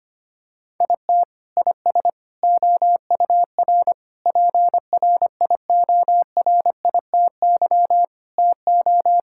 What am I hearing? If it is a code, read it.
Morse code, heard as a keyed tone, about 25 wpm, 700 Hz. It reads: IT IS OUR PRIORITY TO
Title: Sentences from Top 1600 Words 25wpm